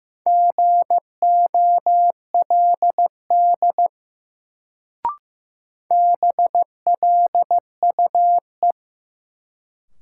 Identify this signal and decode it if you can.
Morse code, heard as a keyed tone, about 16 wpm, 700 Hz. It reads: GOLD BLUE
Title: Sentences from Top 1000 Words - Review Part 3 - New Word Drill - 15wpm